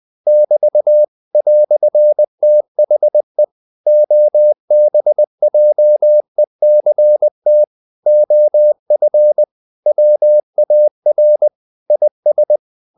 Morse code, heard as a keyed tone, about 20 wpm, 600 Hz. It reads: <BT> "THE OBJECT OF WAR IS
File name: CW / Practice / EnglishQuotations0320